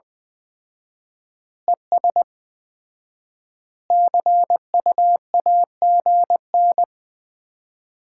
Morse code, heard as a keyed tone, about 21 wpm, 700 Hz. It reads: ES CUAGN
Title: Common QSO Phrases - 4x Word-Spacing - 20wpm